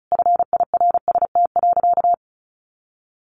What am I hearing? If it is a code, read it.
Morse code, heard as a keyed tone, about 34 wpm, 700 Hz. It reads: FIRST.
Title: Sentences from Top 1100 Words 35wpm